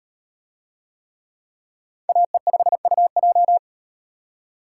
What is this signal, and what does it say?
Morse code, heard as a keyed tone, about 38 wpm, 700 Hz. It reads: AE5UJ